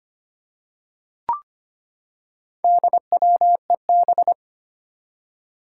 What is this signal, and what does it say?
Morse code, heard as a keyed tone, about 25 wpm, 700 Hz. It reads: DWEB
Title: Four Challenging Letter-Number Combinations 25wpm